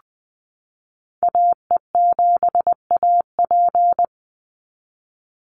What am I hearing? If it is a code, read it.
Morse code, heard as a keyed tone, about 20 wpm, 700 Hz. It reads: AE7AP
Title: NA SOTA Activator Callsigns 20wpm